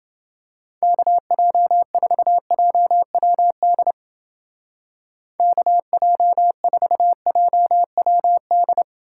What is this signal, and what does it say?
Morse code, heard as a keyed tone, about 30 wpm, 700 Hz. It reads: KJ4JWD KJ4JWD